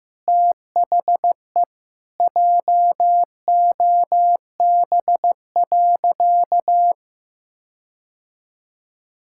Morse code, heard as a keyed tone, about 16 wpm, 700 Hz. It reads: THE JOB.
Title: Sentences from Top 1600 Words 15wpm